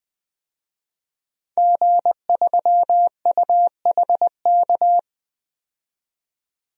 Morse code, heard as a keyed tone, about 21 wpm, 700 Hz. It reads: G3UHK
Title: DXCC-100 Call Signs - 5 Characters - 20wpm